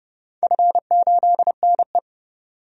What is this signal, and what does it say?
Morse code, heard as a keyed tone, about 30 wpm, 700 Hz. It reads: F8NE
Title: DXCC-100 Call Signs - 4 Characters - 30wpm